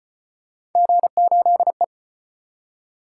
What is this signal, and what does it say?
Morse code, heard as a keyed tone, about 34 wpm, 700 Hz. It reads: G8E